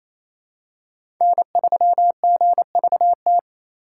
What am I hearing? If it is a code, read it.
Morse code, heard as a keyed tone, about 28 wpm, 700 Hz. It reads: N3GVT
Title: USA Call Signs - 5 Characters - 6x Character Spacing - 28wpm